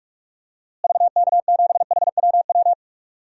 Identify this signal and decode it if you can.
Morse code, heard as a keyed tone, about 41 wpm, 700 Hz. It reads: VK7HWW
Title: DXCC-100 Call Signs - 6 Characters - Repeated 3x - 45wpm